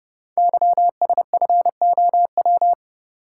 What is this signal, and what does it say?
Morse code, heard as a keyed tone, about 30 wpm, 700 Hz. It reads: YSFOW
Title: Code Groups - 5 Letter:Numbers - 30wpm